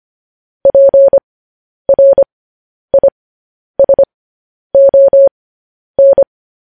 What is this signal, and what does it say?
Morse code, heard as a keyed tone, about 25 wpm, spaced as for 10 wpm, 550 Hz. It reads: PRISON
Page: news-headlines-in-morse-code-at-10-wpm